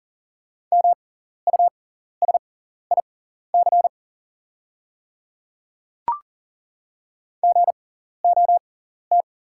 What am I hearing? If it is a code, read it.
Morse code, heard as a keyed tone, about 40 wpm, spaced as for 14 wpm, 690 Hz. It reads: MUSIC GOT
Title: 6x Character Spacing - Top 300 Words - 40wpm